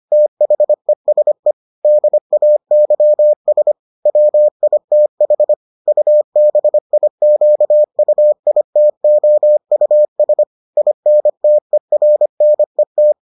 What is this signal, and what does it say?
Morse code, heard as a keyed tone, about 25 wpm, 600 Hz. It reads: THESE DAYS WITH UBIQUITOUS INTERNET